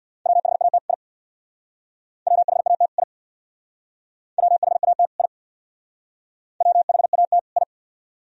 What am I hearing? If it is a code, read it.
Morse code, heard as a keyed tone, about 60 wpm, 700 Hz. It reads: WHAT I WHAT I WHAT I WHAT I
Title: Speed-Racing - Sets of 2 Words 50wpm